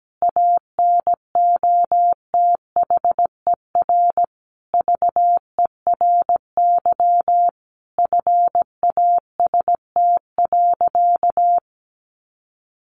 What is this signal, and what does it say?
Morse code, heard as a keyed tone, about 17 wpm, 700 Hz. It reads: ANOTHER VERY FAST.
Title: Sentences from Top 1100 Words 17wpm